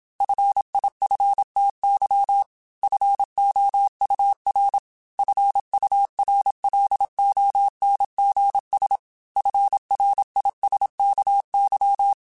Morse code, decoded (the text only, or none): FIFTY FOUR FURLONGS FRISKY